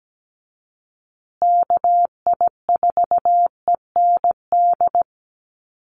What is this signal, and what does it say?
Morse code, heard as a keyed tone, about 17 wpm, 700 Hz. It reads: KI4END
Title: USA Call Signs Repeated 3x 17wpm